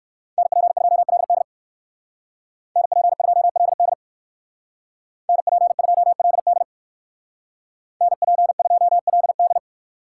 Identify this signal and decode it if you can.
Morse code, heard as a keyed tone, about 57 wpm, 700 Hz. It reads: NP2LD NP2LD NP2LD NP2LD